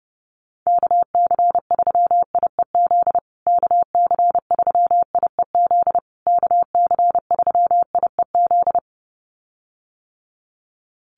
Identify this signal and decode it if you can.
Morse code, heard as a keyed tone, about 30 wpm, 700 Hz. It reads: KC3IEZ KC3IEZ KC3IEZ
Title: USA Call Signs Repeated 3x 30wpm